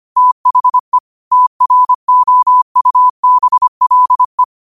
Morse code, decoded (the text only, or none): THE TROUBLE